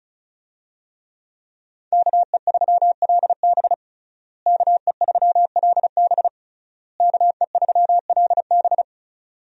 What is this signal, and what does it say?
Morse code, heard as a keyed tone, about 36 wpm, 700 Hz. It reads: KE3LB KE3LB KE3LB
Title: USA Call Signs - 5 Characters - Repeated 3x - 35wpm